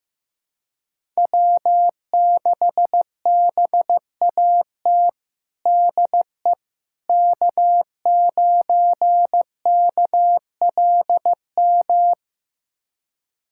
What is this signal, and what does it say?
Morse code, heard as a keyed tone, about 15 wpm, 700 Hz. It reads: W6BAT DE K9KLM
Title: USA Callsign DE Callsign - 5 Characters - 15wpm